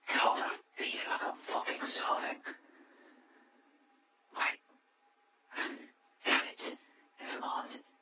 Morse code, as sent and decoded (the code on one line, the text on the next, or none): none